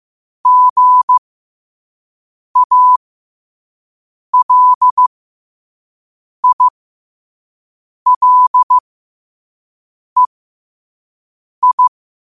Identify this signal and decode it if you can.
Morse code, heard as a keyed tone, about 15 wpm, spaced as for 5 wpm, 1000 Hz. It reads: GALILEI
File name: qotdfiles / 5wpm